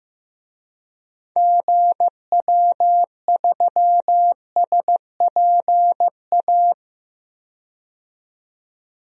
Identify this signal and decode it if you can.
Morse code, heard as a keyed tone, about 16 wpm, 700 Hz. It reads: GW3SPA
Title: DXCC-100 Call Signs - 6 Characters - 15wpm